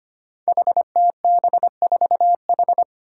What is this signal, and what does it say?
Morse code, heard as a keyed tone, about 25 wpm, 700 Hz. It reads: HTB4H
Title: Five Challenging Letter-Number Combinations 25wpm